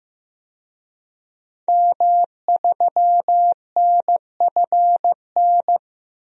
Morse code, heard as a keyed tone, about 15 wpm, 700 Hz. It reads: M3NFN
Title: DXCC-100 Call Signs - 5 Characters - 15wpm